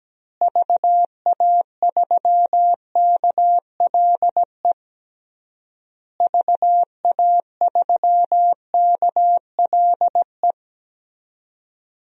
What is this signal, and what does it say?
Morse code, heard as a keyed tone, about 17 wpm, 700 Hz. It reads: VA3KLE VA3KLE
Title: DXCC-100 Call Signs - 6 Characters - Repeated 3x - 17wpm